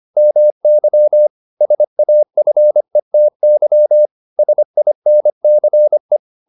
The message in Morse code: MY SAFETY SINCE

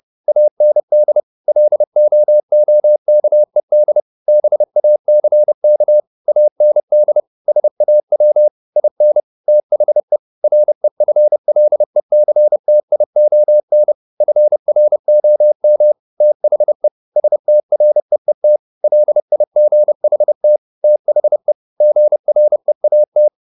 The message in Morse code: AND LOOKED BACK AND SAW IN THE REFLECTION FROM THE STREET LIGHT THE GREAT

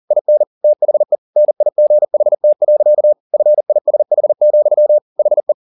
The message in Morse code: IN THE NIGHT. FISH, HE

